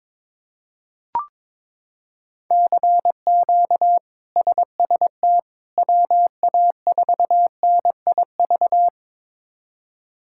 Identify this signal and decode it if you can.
Morse code, heard as a keyed tone, about 22 wpm, 700 Hz. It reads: CQ SST WA4NIV
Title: K1USN SST QSO Practice - 22wpm